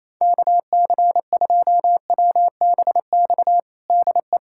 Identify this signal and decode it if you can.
Morse code, heard as a keyed tone, about 28 wpm, 700 Hz. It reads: KC2WBX DE
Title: USA Callsign DE Callsign - 6 Characters - 28wpm